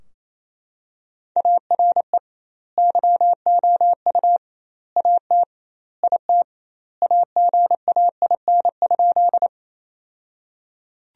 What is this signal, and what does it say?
Morse code, heard as a keyed tone, about 28 wpm, 700 Hz. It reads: ARE YOU AT IT AGAIN?
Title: 2x Word-Spacing - Sentences from Top 200 Words - 28wpm